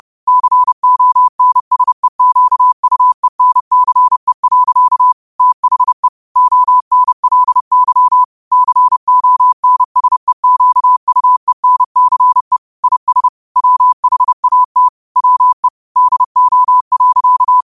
Morse code, heard as a keyed tone, about 30 wpm, 1000 Hz. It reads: CONSEQUENCE. THE ONLY CONSEQUENCE IS WHAT WE DO.
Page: New quotes every day in morse code at 30 Words per minute.